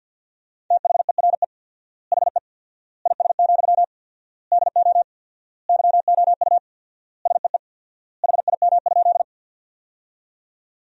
Morse code, heard as a keyed tone, about 50 wpm, 700 Hz. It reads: THERE HE IS, DO YOU SEE HIM?
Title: 4x Word-Spacing - Sentences from Top 100 Words - 50wpm